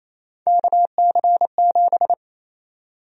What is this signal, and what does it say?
Morse code, heard as a keyed tone, about 28 wpm, 700 Hz. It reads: KC7
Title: USA Call Sign Prefixes - 28wpm